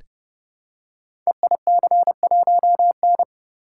Morse code, heard as a keyed tone, about 30 wpm, 700 Hz. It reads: EIC1N
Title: Code Groups - 5 Letter:Numbers - 30wpm